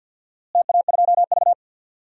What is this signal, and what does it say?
Morse code, heard as a keyed tone, about 52 wpm, 700 Hz. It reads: TA2V
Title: DXCC-100 Call Signs - 4 Characters - Repeated 3x - 50wpm